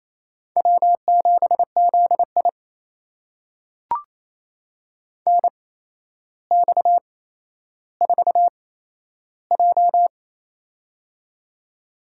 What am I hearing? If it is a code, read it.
Morse code, heard as a keyed tone, about 28 wpm, spaced as for 12 wpm, 700 Hz. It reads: W7ZI NX4J
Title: USA Call Signs - 4 Characters - 8x Character Spacing - 28wpm